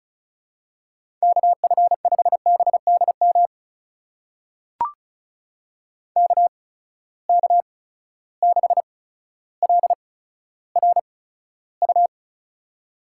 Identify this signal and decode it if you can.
Morse code, heard as a keyed tone, about 36 wpm, spaced as for 15 wpm, 700 Hz. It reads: KF5BDM KK6LRU
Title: USA Call Signs - 6 Characters - 8x Character Spacing - 35wpm